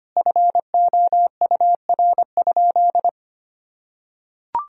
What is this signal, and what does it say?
Morse code, heard as a keyed tone, about 25 wpm, 700 Hz. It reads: FOUR?
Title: Sentences from Top 1000 Words - Review Part 1 - 25wpm